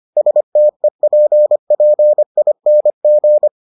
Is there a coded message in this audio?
Yes: Morse code from 0.2 to 3.5 s, heard as a keyed tone, about 25 wpm, 600 Hz. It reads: STEPPING